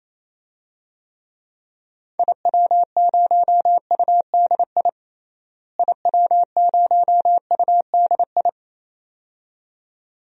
Morse code, heard as a keyed tone, about 28 wpm, 700 Hz. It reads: IW0UDI IW0UDI
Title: DXCC-100 Call Signs - 6 Characters - Repeated 2x - 28wpm